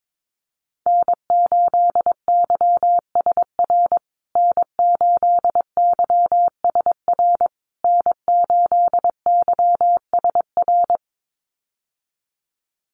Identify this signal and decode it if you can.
Morse code, heard as a keyed tone, about 22 wpm, 700 Hz. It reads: N8YSR N8YSR N8YSR
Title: USA Call Signs Repeated 3x 22wpm